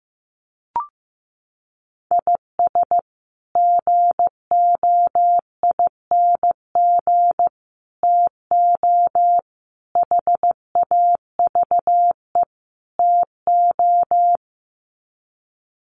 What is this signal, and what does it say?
Morse code, heard as a keyed tone, about 15 wpm, 700 Hz. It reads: IS GOING TO HAVE TO
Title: Sets of 5 Words 15wpm